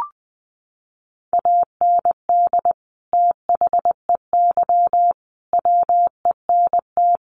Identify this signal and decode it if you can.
Morse code, heard as a keyed tone, about 20 wpm, 700 Hz. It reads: AND THEY WENT
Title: Sentences from Top 200 Words 20wpm